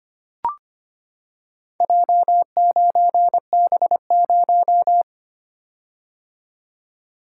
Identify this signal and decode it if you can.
Morse code, heard as a keyed tone, about 25 wpm, 700 Hz. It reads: J9B0
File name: practice / 25wpm / Four Letter-Number Combinations - 25wpm